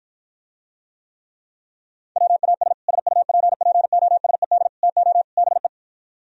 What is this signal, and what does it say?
Morse code, heard as a keyed tone, about 53 wpm, 700 Hz. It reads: WAS SUPPOSED TO BE